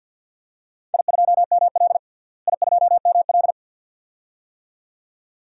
Morse code, heard as a keyed tone, about 46 wpm, 700 Hz. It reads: I2ML I2ML
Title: DXCC-100 Call Signs - 4 Characters - Repeated 2x - 50wpm